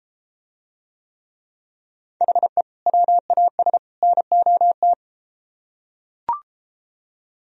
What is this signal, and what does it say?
Morse code, heard as a keyed tone, about 34 wpm, 700 Hz. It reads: HE WAS NOT